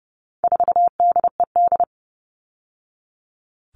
Morse code, heard as a keyed tone, about 30 wpm, 700 Hz. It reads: 4DED
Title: All Cut Numbers - 4-Digit 30wpm